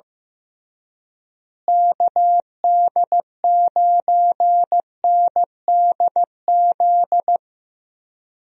Morse code, heard as a keyed tone, about 15 wpm, 700 Hz. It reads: KD9NDZ
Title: USA Call Signs - 6 Characters - Repeated 3x - 15wpm